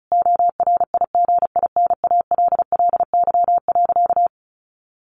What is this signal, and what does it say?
Morse code, heard as a keyed tone, about 34 wpm, 700 Hz. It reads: ORIGINALLY.
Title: Sentences from Top 1500 Words 35wpm